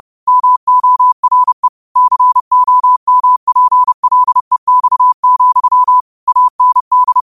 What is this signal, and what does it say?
Morse code, heard as a keyed tone, about 30 wpm, 1000 Hz. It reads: MORE COMPLEX, AND
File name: qotdfiles / 30wpm